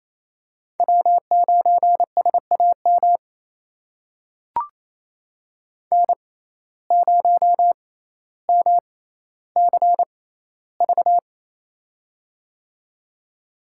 Morse code, heard as a keyed tone, about 27 wpm, spaced as for 14 wpm, 700 Hz. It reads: W9SAM N0MCV
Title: USA Call Signs - 5 Characters - 6x Character Spacing - 28wpm